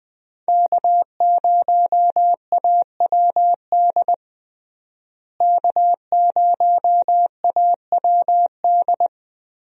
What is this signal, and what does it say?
Morse code, heard as a keyed tone, about 20 wpm, 700 Hz. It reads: K0AWD K0AWD
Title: USA Call Signs - 5 Characters - Repeated 2x - 20wpm